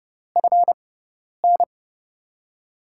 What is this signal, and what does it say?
Morse code, heard as a keyed tone, about 32 wpm, spaced as for 10 wpm, 700 Hz. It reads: FN
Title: USA Call Signs - 6 Characters - 6x Character Spacing - 30wpm